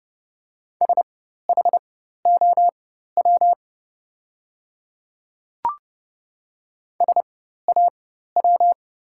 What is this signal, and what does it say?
Morse code, heard as a keyed tone, about 31 wpm, spaced as for 14 wpm, 690 Hz. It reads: SHOW SAW
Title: 4x Character Spacing - Top 500 Words - 30wpm